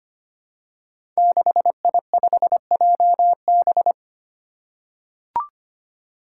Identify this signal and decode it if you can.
Morse code, heard as a keyed tone, about 25 wpm, 700 Hz. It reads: 6I5JB
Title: Five Challenging Letter-Number Combinations 25wpm